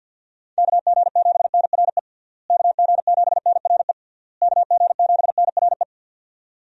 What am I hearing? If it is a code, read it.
Morse code, heard as a keyed tone, about 50 wpm, 700 Hz. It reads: KG7NRE KG7NRE KG7NRE
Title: USA Call Signs - 6 Characters - Repeated 3x - 50wpm